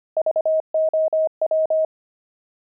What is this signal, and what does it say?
Morse code, heard as a keyed tone, about 25 wpm, 630 Hz. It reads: VOW
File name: pangram-fs-25-cs-25-zany